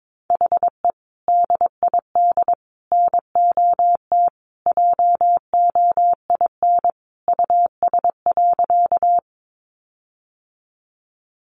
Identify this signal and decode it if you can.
Morse code, heard as a keyed tone, about 22 wpm, 700 Hz. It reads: HE DID NOT JOIN US.
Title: Sentences from Top 1400 Words 22wpm